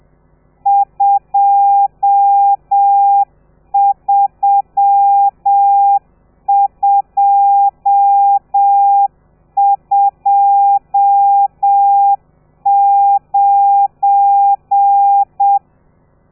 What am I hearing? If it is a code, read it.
Morse code, heard as a keyed tone, about 7 wpm, 800 Hz. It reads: 23229
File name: telegrafia / i2tfj / Lez12 30